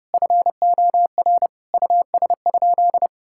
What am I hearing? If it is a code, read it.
Morse code, heard as a keyed tone, about 30 wpm, 700 Hz. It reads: FOR US?
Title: Sentences from Top 1000 Words - Review Part 5 - 30wpm